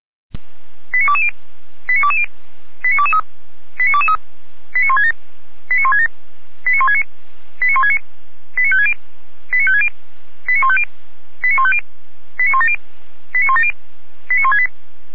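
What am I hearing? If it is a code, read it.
Five-tone selective call (ZVEI), heard as quick runs of five tones, tones of about 70 ms. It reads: 89220 89220 89223 89223 89167 89167 89169 89169 89560 89560 89150 89150 89170 89170 89168
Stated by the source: Five-tone ZVEI selective call received on an SDR, from audio.f85360.feuerwehr-orlamuende.org (Feuerwehr Orlamünde fire brigade).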